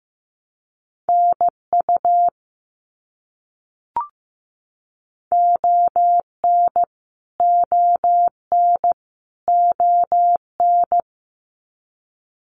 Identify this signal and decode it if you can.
Morse code, heard as a keyed tone, about 15 wpm, 700 Hz. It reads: NU ON ON ON
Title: Canadian Provinces and Territories - Repeated 3x - 15wpm